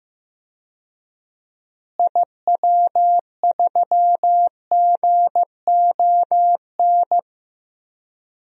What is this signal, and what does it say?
Morse code, heard as a keyed tone, about 15 wpm, 700 Hz. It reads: IW3GON
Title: DXCC-100 Call Signs - 6 Characters - Repeated 3x - 15wpm